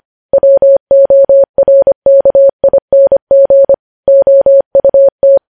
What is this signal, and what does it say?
Morse code, heard as a keyed tone, about 25 wpm, 550 Hz. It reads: WORKING OUT